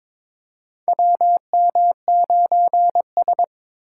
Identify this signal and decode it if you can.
Morse code, heard as a keyed tone, about 22 wpm, 700 Hz. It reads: WM9S
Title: USA Call Signs - 4x Character Spacing - 22wpm